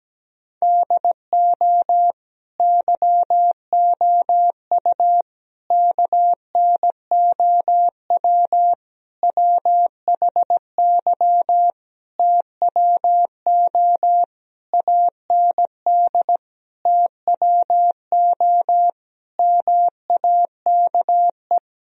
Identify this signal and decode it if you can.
Morse code, heard as a keyed tone, about 17 wpm, 700 Hz. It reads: DO YOU KNOW WHY TWO AND TWO MAKE
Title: Sentences from Top 1000 Words - Review Part 1 - 17wpm